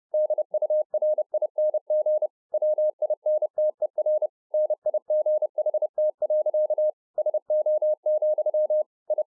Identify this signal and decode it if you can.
Morse code, heard as a keyed tone, about 30 wpm, 610 Hz. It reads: DURING WINTER NIGHT. SO, I